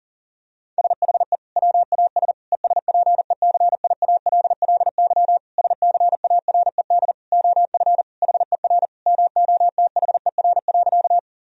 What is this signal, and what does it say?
Morse code, heard as a keyed tone, about 40 wpm, 700 Hz. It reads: SHE WAS ESPECIALLY SCARED OF HER MOTHER.
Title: Sentences from Top 1000 Words - Review Part 4 - 40wpm